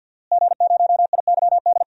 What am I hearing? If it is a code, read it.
Morse code, heard as a keyed tone, about 53 wpm, 700 Hz. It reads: G0IYD
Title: DXCC-100 Call Signs - 5 Characters - Repeated 3x - 50wpm